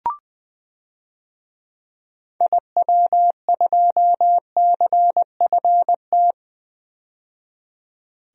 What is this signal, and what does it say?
Morse code, heard as a keyed tone, about 20 wpm, 700 Hz. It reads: IW2CFT
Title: DXCC-100 Call Signs - 6 Characters - 20wpm